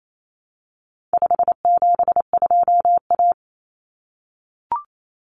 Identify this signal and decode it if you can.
Morse code, heard as a keyed tone, about 28 wpm, 700 Hz. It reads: 572A
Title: Common Cut Numbers - 4-Digit 28wpm